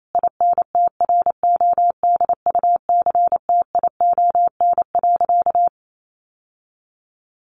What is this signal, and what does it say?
Morse code, heard as a keyed tone, about 28 wpm, 700 Hz. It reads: INTRODUCTION.
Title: Sentences from Top 1000 Words 28wpm